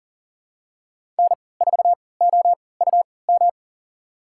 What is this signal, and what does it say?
Morse code, heard as a keyed tone, about 40 wpm, spaced as for 23 wpm, 700 Hz. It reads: N4OUM
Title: USA Call Signs - 3x Character Spacing - 40wpm